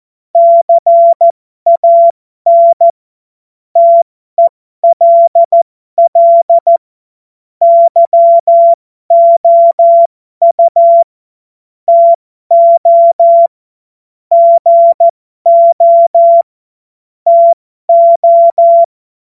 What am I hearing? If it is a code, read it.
Morse code, heard as a keyed tone, about 14 wpm, spaced as for 12 wpm, 680 Hz. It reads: CAN TELL YOU TO GO TO
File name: Morsekurs / Witze